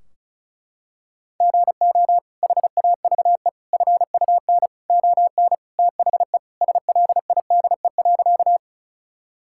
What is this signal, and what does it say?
Morse code, heard as a keyed tone, about 34 wpm, 700 Hz. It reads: GO HAVE FUN ON THE SLIDE.
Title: Sentences from Top 1600 Words 35wpm